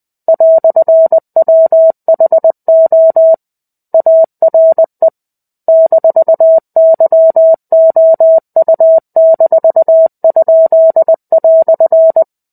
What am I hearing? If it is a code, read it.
Morse code, heard as a keyed tone, about 20 wpm, 650 Hz. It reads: "WHO ARE -YOU-?"
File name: 05-Chapter-05 20 WPM